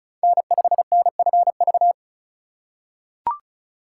Morse code, heard as a keyed tone, about 36 wpm, 690 Hz. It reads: N5NFV
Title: USA Call Signs - 5 Characters - 2x Character Spacing - 35wpm